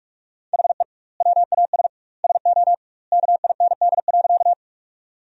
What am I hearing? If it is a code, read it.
Morse code, heard as a keyed tone, about 46 wpm, 700 Hz. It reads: HE WAS SO KIND.
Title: 2x Word-Spacing - Sentences from Top 200 Words - 45wpm